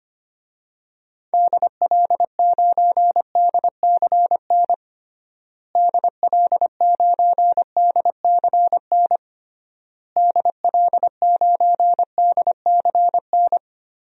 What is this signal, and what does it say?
Morse code, heard as a keyed tone, about 25 wpm, 700 Hz. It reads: DL9DCN DL9DCN DL9DCN
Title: DXCC-100 Call Signs - 6 Characters - Repeated 3x - 25wpm